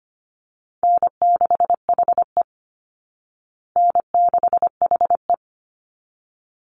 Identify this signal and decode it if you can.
Morse code, heard as a keyed tone, about 25 wpm, 700 Hz. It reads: N6HE N6HE
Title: Easy Call Signs Repeated 2x - 25wpm